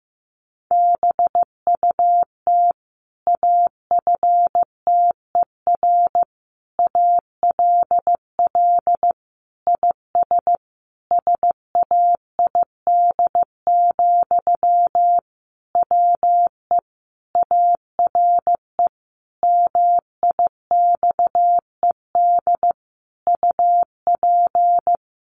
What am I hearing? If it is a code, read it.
Morse code, heard as a keyed tone, about 15 wpm, 700 Hz. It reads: BUT AFTER ALL IS SAID, WE ARE MIXED UP
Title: Sentences from Top 1500 Words 15wpm